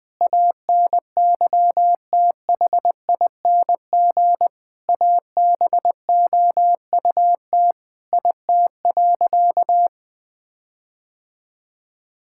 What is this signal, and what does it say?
Morse code, heard as a keyed tone, about 21 wpm, 700 Hz. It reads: ANYTHING ABOUT IT.
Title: Sentences from Top 1600 Words 20wpm